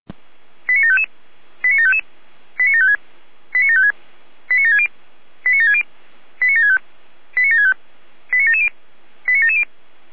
Five-tone selective call (ZVEI), heard as quick runs of five tones, tones of about 70 ms. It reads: 89755 89755 89756 89756 89760 89760 89765 89765 89770 89770
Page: ZVEI